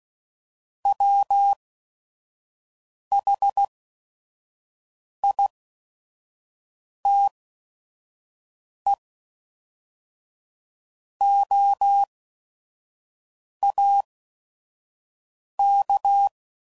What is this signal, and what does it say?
Morse code, heard as a keyed tone, about 16 wpm, spaced as for 5 wpm, 800 Hz. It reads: WHITEOAK
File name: CW Code Groups @5WPM / fromSM's / Cities